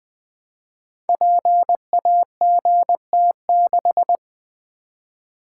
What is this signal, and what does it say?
Morse code, heard as a keyed tone, about 20 wpm, 700 Hz. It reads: PAGT6
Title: Code Groups - 5 Letter:Numbers - 20wpm